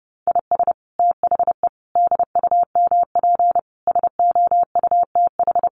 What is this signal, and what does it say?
Morse code, heard as a keyed tone, about 30 wpm, 700 Hz. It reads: IS THE DUMP SOUTH
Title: Sentences from Top 1300 Words 30wpm